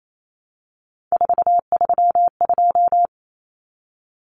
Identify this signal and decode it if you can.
Morse code, heard as a keyed tone, about 28 wpm, 700 Hz. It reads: 432
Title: Common Cut Numbers - 3-Digit 28wpm